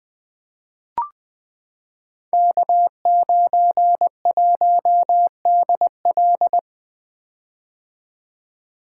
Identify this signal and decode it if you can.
Morse code, heard as a keyed tone, about 20 wpm, 700 Hz. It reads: K91DL
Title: Code Groups - 5 Letter:Numbers - 20wpm